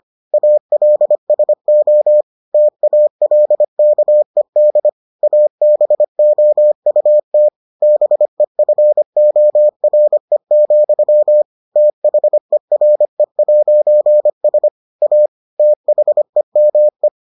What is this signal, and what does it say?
Morse code, heard as a keyed tone, about 25 wpm, 600 Hz. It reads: ALSO TALKED ABOUT BEFORE, THERE'S A THEME